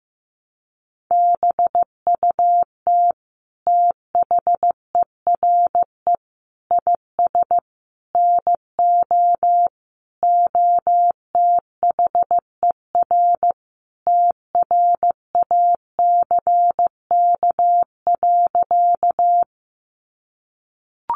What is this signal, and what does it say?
Morse code, heard as a keyed tone, about 15 wpm, 700 Hz. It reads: BUT THERE IS NO OTHER TRACK.
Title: Sentences from Top 1100 Words 15wpm